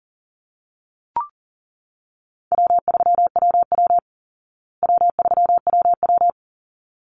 Morse code, heard as a keyed tone, about 40 wpm, 700 Hz. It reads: W3WW W3WW
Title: Easy Call Signs Repeated 2x - 40wpm